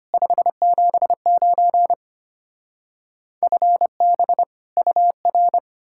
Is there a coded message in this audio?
Yes: Morse code from 0.1 to 5.6 s, heard as a keyed tone, about 33 wpm, 700 Hz. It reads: 579 FB UR